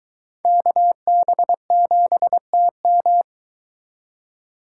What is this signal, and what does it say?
Morse code, heard as a keyed tone, about 23 wpm, 700 Hz. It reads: KB7TM